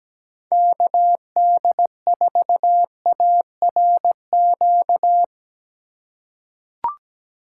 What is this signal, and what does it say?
Morse code, heard as a keyed tone, about 17 wpm, 700 Hz. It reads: KD4ARQ
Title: USA Call Signs - 6 Characters - 2x Character Spacing - 17wpm